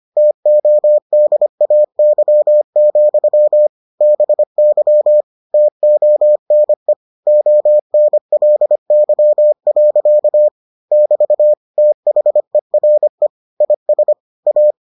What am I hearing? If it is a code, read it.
Morse code, heard as a keyed tone, about 25 wpm, 600 Hz. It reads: TODAY, BY TONE ONLY. <BT> THERE IS A